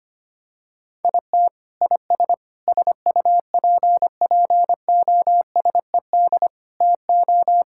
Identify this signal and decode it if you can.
Morse code, heard as a keyed tone, about 25 wpm, 700 Hz. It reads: IT IS SUPPOSED TO
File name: practice / 25wpm / Sentences with 3-Word Phrases - 25wpm